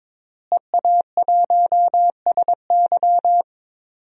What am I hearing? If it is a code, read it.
Morse code, heard as a keyed tone, about 22 wpm, 700 Hz. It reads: EA1SY
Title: DXCC-100 Call Signs - 5 Characters - Repeated 3x - 22wpm